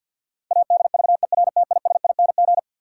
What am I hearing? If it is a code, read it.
Morse code, heard as a keyed tone, about 47 wpm, 700 Hz. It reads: ADVERTISING
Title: Sentences from Top 1000 Words - Review Part 6 - New Word Drill - 50wpm